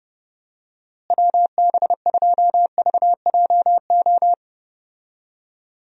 Morse code, heard as a keyed tone, about 31 wpm, 700 Hz. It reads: WB2VJO
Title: USA Call Signs - 6 Characters - Repeated 2x - 30wpm